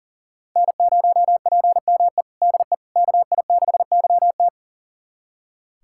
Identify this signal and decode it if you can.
Morse code, heard as a keyed tone, about 37 wpm, 700 Hz. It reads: N0PME DE KI6YT
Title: USA Callsign DE Callsign - 5 Characters - 40wpm